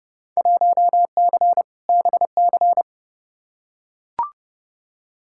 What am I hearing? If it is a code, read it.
Morse code, heard as a keyed tone, about 30 wpm, 700 Hz. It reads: 1C BC
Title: Field Day Exchanges Encore - 30wpm